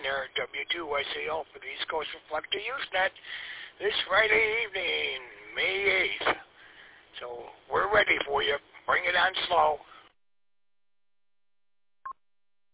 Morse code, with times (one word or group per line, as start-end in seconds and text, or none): none